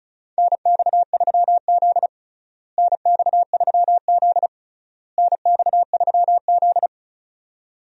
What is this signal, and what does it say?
Morse code, heard as a keyed tone, about 34 wpm, 700 Hz. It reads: NX3Z NX3Z NX3Z
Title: USA Call Signs - 4 Characters - Repeated 3x - 35wpm